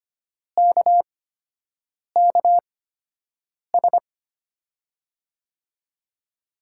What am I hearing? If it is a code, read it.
Morse code, heard as a keyed tone, about 25 wpm, spaced as for 7 wpm, 700 Hz. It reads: KKS
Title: USA Call Signs - 8x Character Spacing - 25wpm